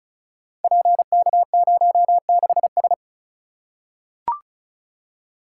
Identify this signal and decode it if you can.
Morse code, heard as a keyed tone, about 36 wpm, 700 Hz. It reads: PK06S
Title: Code Groups - 5 Letter:Numbers - 35wpm